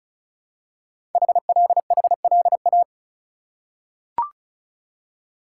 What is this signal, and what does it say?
Morse code, heard as a keyed tone, about 36 wpm, 700 Hz. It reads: HLHLA
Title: Code Groups - 5 Letters - 35wpm